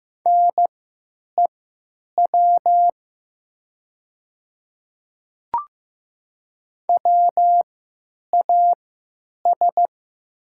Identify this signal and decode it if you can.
Morse code, heard as a keyed tone, about 15 wpm, spaced as for 9 wpm, 700 Hz. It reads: NEW WAS
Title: 3x Character Spacing - Top 200 Words - 15wpm